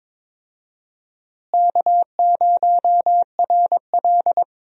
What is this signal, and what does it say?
Morse code, heard as a keyed tone, about 23 wpm, 700 Hz. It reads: K0RL
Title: USA Call Signs - 4 Characters - Repeated 2x - 22wpm